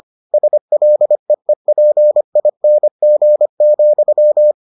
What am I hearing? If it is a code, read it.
Morse code, heard as a keyed tone, about 25 wpm, 600 Hz. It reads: SLEEPING,